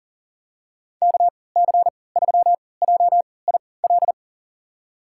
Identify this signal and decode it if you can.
Morse code, heard as a keyed tone, about 40 wpm, spaced as for 23 wpm, 700 Hz. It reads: KC3JIL
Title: USA Call Signs - 6 Characters - 3x Character Spacing - 40wpm